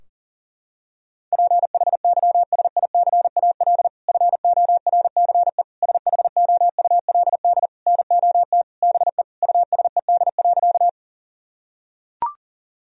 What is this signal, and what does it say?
Morse code, heard as a keyed tone, about 40 wpm, 700 Hz. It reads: PHYSICAL FORCE SHOULD NOT BE USED.
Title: Sentences from Top 1000 Words - Review Part 4 - 40wpm